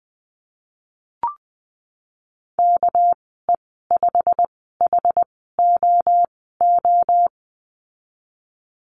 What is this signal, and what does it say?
Morse code, heard as a keyed tone, about 20 wpm, spaced as for 15 wpm, 700 Hz. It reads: KE5HOO
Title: 200 Lesson 32g - Letter k - Call Signs 20wpm